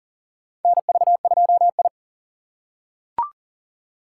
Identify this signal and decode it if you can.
Morse code, heard as a keyed tone, about 40 wpm, 700 Hz. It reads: NV2I
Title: USA Call Signs - 3x Character Spacing - 40wpm